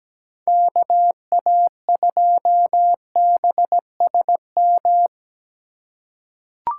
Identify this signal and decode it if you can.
Morse code, heard as a keyed tone, about 17 wpm, 700 Hz. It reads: KA2BSM
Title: USA Call Signs - 6 Characters - 4x Character Spacing - 17wpm